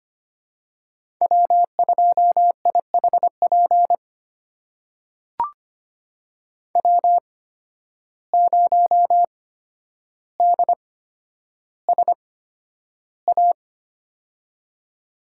Morse code, heard as a keyed tone, about 25 wpm, spaced as for 11 wpm, 700 Hz. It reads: W2IHP W0DSA
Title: USA Call Signs - 5 Characters - 8x Character Spacing - 25wpm